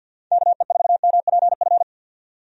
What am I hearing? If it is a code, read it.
Morse code, heard as a keyed tone, about 52 wpm, 700 Hz. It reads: KE4MPF
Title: USA Call Signs - 6 Characters - 50wpm